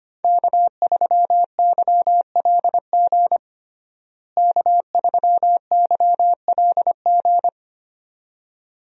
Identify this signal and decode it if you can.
Morse code, heard as a keyed tone, about 25 wpm, 700 Hz. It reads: K3YLG K3YLG
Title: USA Call Signs - 5 Characters - Repeated 2x - 25wpm